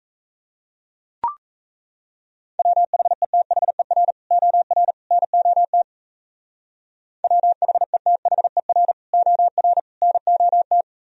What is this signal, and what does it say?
Morse code, heard as a keyed tone, about 34 wpm, 700 Hz. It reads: WHETHER OR NOT WHETHER OR NOT
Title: Speed-Racing - Sets of 3 Words 28wpm